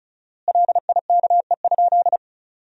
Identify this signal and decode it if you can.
Morse code, heard as a keyed tone, about 34 wpm, 700 Hz. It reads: LIKE?
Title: 4x Word-Spacing - Sentences from Top 100 Words - 35wpm